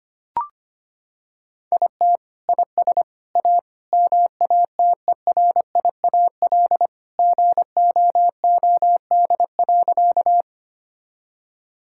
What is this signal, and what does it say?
Morse code, heard as a keyed tone, about 25 wpm, 700 Hz. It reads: IT IS A MATERIAL GOOD.
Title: Sentences from Top 1000 Words - Review Part 5 - 25wpm